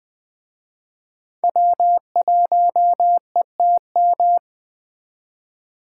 Morse code, heard as a keyed tone, about 20 wpm, 700 Hz. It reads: W1ETM